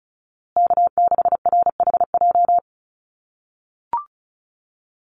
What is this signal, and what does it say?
Morse code, heard as a keyed tone, about 34 wpm, 700 Hz. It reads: K6RHJ
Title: USA Call Signs Repeated 3x 35wpm